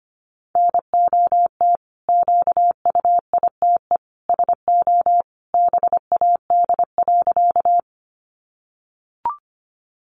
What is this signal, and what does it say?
Morse code, heard as a keyed tone, about 25 wpm, 700 Hz. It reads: NOT QUITE SO BAD.
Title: Sentences from Top 700 Words 25wpm